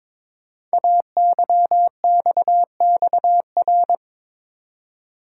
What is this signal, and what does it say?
Morse code, heard as a keyed tone, about 23 wpm, 700 Hz. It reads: AYXXR
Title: Code Groups - 5 Letters - 22wpm